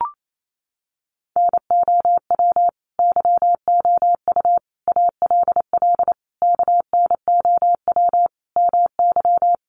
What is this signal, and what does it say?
Morse code, heard as a keyed tone, about 28 wpm, 700 Hz. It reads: NOW YOU ALL KNOW MY
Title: Sentences from Top 1200 Words 28wpm